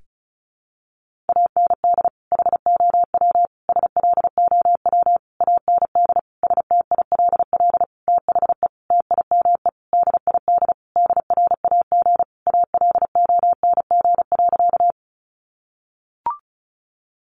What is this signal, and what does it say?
Morse code, heard as a keyed tone, about 34 wpm, 700 Hz. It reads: AND HOW SLOW AND STILL THE TIME DID DRAG ALONG.
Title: Sentences from Top 1500 Words 35wpm